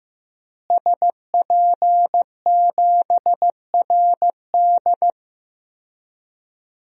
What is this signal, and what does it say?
Morse code, heard as a keyed tone, about 15 wpm, 700 Hz. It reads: SP7RD
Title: DXCC-100 Call Signs - 5 Characters - Repeated 2x - 15wpm